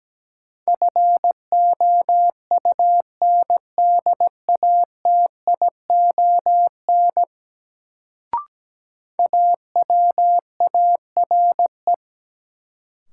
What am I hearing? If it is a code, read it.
Morse code, heard as a keyed tone, about 17 wpm, 700 Hz. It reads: FOUNDATION AWARE
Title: Sentences from Top 1200 Words - New Word Drill Alt - 17wpm